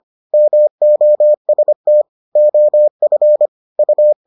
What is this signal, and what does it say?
Morse code, heard as a keyed tone, about 25 wpm, 600 Hz. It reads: MOST OF U